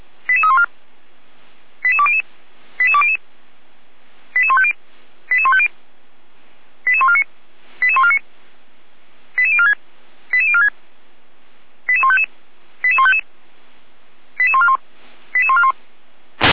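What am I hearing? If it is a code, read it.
Five-tone selective call (ZVEI), heard as quick runs of five tones, tones of about 70 ms. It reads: 88314 88200 88200 88150 88150 88149 88149 88046 88046 88155 88155 88141 88141
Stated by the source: ZVEI five-tone call received by SDR.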